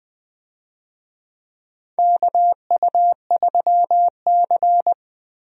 Morse code, heard as a keyed tone, about 20 wpm, 700 Hz. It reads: KU3C
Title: USA Call Signs - 4 Characters - Repeated 3x - 20wpm